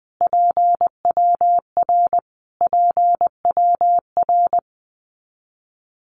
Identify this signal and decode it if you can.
Morse code, heard as a keyed tone, about 20 wpm, 700 Hz. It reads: PWR PWR
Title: Abbreviations Repeated 3x - 20wpm